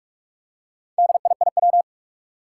Morse code, heard as a keyed tone, about 44 wpm, 700 Hz. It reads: DIIW
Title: Four Challenging Letter-Number Combinations 45wpm